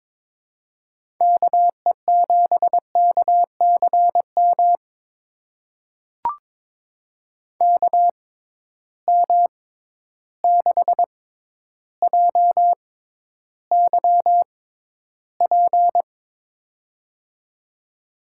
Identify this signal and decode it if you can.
Morse code, heard as a keyed tone, about 22 wpm, spaced as for 11 wpm, 700 Hz. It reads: KE7KCM KM6JYP
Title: USA Call Signs - 6 Characters - 6x Character Spacing - 22wpm